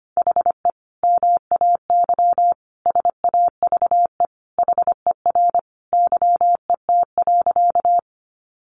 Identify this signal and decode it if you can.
Morse code, heard as a keyed tone, about 25 wpm, 700 Hz. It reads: HE MAY SAVE HER YET.
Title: Sentences from Top 1100 Words 25wpm